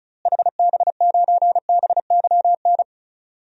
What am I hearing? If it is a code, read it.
Morse code, heard as a keyed tone, about 36 wpm, 700 Hz. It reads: HB9BYN
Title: DXCC-100 Call Signs - 6 Characters - Repeated 2x - 35wpm